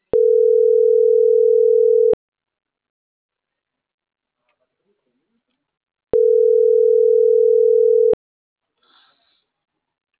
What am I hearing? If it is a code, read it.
Ringback tone (North America), 440 + 480 Hz.